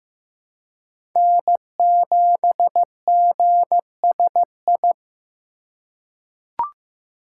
Morse code, heard as a keyed tone, about 15 wpm, 700 Hz. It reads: N7GSI
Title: USA Call Signs - 5 Characters - 4x Character Spacing - 15wpm